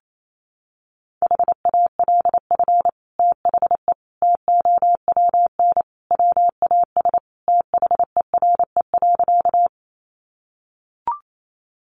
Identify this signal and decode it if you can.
Morse code, heard as a keyed tone, about 28 wpm, 700 Hz. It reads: HALF THE TOWN WAS THERE.
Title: Sentences from Top 500 Words 28wpm